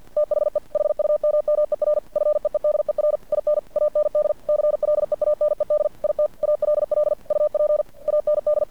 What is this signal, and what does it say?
Morse code, heard as a keyed tone, about 50 wpm, 600 Hz. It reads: THE HUMMER FEEDER IN AND CLEANED IT ALL UP AND
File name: archived / CRSnet